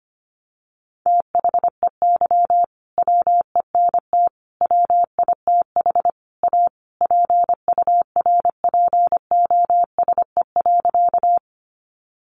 Morse code, heard as a keyed tone, about 25 wpm, 700 Hz. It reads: THEY WENT WITH A PURPOSE.
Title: Sentences from Top 800 Words 25wpm